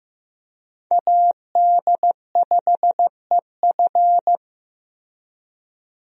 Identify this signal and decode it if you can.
Morse code, heard as a keyed tone, about 15 wpm, 700 Hz. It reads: AD5EF
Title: USA Call Signs - 5 Characters - Repeated 2x - 15wpm